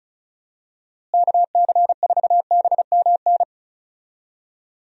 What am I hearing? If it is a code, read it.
Morse code, heard as a keyed tone, about 34 wpm, 700 Hz. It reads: KC4BMN
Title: USA Call Signs - 6 Characters - Repeated 3x - 35wpm